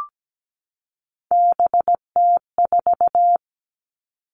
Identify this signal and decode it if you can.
Morse code, heard as a keyed tone, about 17 wpm, 700 Hz. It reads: BT4
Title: Three Challenging Letter-Number Combinations 17wpm